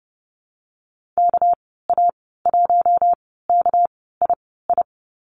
Morse code, heard as a keyed tone, about 30 wpm, spaced as for 17 wpm, 700 Hz. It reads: KA1KII
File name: Lesson 37g - Letter x - Call Signs 30wpm